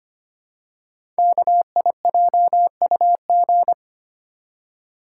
Morse code, heard as a keyed tone, about 25 wpm, 700 Hz. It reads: KIJUG
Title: Code Groups - 5 Letters - 25wpm